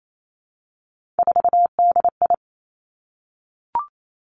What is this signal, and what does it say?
Morse code, heard as a keyed tone, about 28 wpm, 700 Hz. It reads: 4DI
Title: Three Challenging Letter-Number Combinations 28wpm